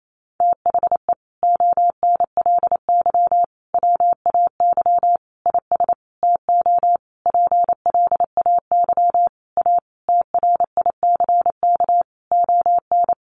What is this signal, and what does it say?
Morse code, heard as a keyed tone, about 28 wpm, 700 Hz. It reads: THE ONLY WAY IS TO PLAY A TRICK ON
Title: Sentences from Top 1500 Words 28wpm